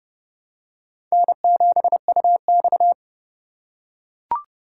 Morse code, heard as a keyed tone, about 30 wpm, 700 Hz. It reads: N7UX
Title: USA Call Signs - 4 Characters - 4x Character Spacing - 30wpm